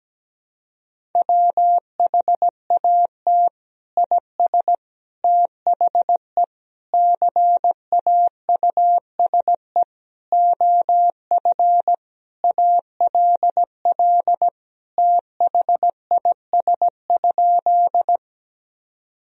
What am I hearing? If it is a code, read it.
Morse code, heard as a keyed tone, about 17 wpm, 700 Hz. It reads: WHAT IS THE CAUSE OF ALL THIS?
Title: Sentences from Top 1000 Words - Review Part 1 - 17wpm